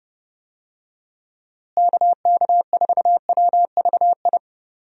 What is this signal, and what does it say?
Morse code, heard as a keyed tone, about 32 wpm, 700 Hz. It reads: KK4WVI
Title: USA Call Signs - 6 Characters - Repeated 3x - 30wpm